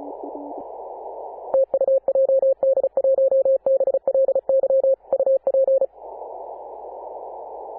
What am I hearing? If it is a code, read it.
Morse code, heard as a keyed tone, about 34 wpm, 540 Hz. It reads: TU JD1BLY UP
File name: Apr/29 2042z JD1BLY 14.011MHz CW